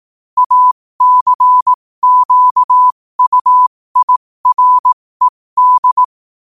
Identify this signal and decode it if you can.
Morse code, heard as a keyed tone, about 18 wpm, spaced as for 15 wpm, 1000 Hz. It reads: ACQUIRED
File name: qotdfiles / 15wpm